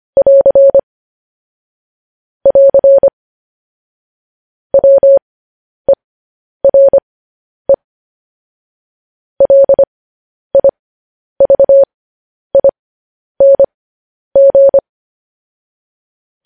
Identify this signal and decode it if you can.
Morse code, heard as a keyed tone, about 25 wpm, spaced as for 10 wpm, 550 Hz. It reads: <AR> <AR> WERE LIVING